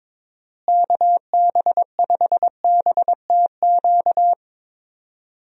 Morse code, heard as a keyed tone, about 22 wpm, 700 Hz. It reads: KB5BTQ
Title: USA Call Signs - 6 Characters - Repeated 3x - 22wpm